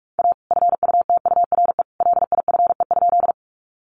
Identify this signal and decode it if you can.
Morse code, heard as a keyed tone, about 44 wpm, 700 Hz. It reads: A FUTURE LIFE?
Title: Sentences from Top 700 Words 45wpm